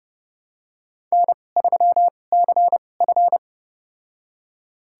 Morse code, heard as a keyed tone, about 30 wpm, spaced as for 22 wpm, 700 Hz. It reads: N3CF
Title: USA Call Signs - 4 Characters - 2x Character Spacing - 30wpm